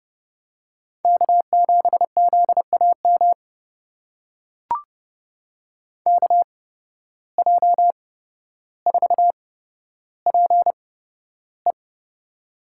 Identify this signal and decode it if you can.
Morse code, heard as a keyed tone, about 30 wpm, spaced as for 13 wpm, 700 Hz. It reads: K7ZAM KJ4PE
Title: USA Call Signs - 5 Characters - 8x Character Spacing - 30wpm